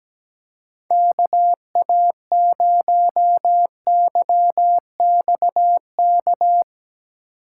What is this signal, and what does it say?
Morse code, heard as a keyed tone, about 17 wpm, 700 Hz. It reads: KA0YXK
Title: USA Call Signs - 3x Character Spacing - 17wpm